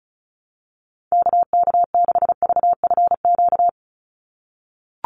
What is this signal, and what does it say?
Morse code, heard as a keyed tone, about 34 wpm, 700 Hz. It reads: KK6VFQ
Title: USA Call Signs 35wpm